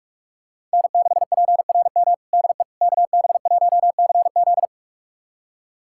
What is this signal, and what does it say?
Morse code, heard as a keyed tone, about 41 wpm, 700 Hz. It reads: N6PRM DE KB1CZ
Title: USA Callsign DE Callsign - 5 Characters - 45wpm